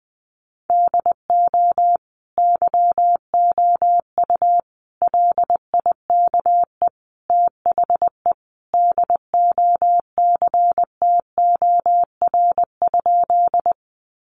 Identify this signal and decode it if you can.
Morse code, heard as a keyed tone, about 20 wpm, 700 Hz. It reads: DO YOU LIKE THE DOCTOR?
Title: Sentences from Top 1500 Words 20wpm